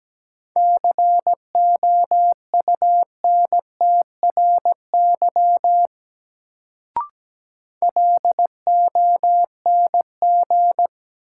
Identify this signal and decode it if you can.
Morse code, heard as a keyed tone, about 17 wpm, 700 Hz. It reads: COUNTRY LONG
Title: Sets of 2 Words - New Word Drill - 17wpm